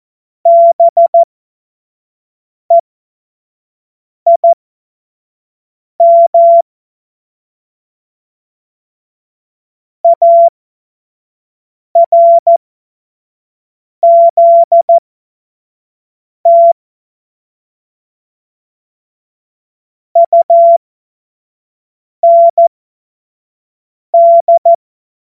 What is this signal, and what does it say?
Morse code, heard as a keyed tone, about 14 wpm, spaced as for 5 wpm, 680 Hz. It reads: BEIM ARZT UND
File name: Morsekurs / Witze / Witz 09 April 2024 25 F70